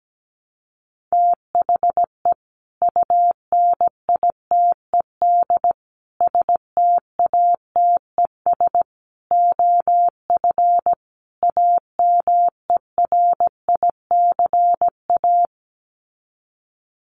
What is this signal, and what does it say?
Morse code, heard as a keyed tone, about 17 wpm, 700 Hz. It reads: THE UNITED STATES OF AMERICA
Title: Sets of 5 Words 17wpm